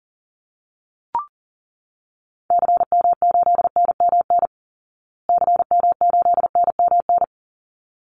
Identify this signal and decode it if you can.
Morse code, heard as a keyed tone, about 40 wpm, 700 Hz. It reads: CM8NMN CM8NMN
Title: Call Signs Repeated 3x - 40wpm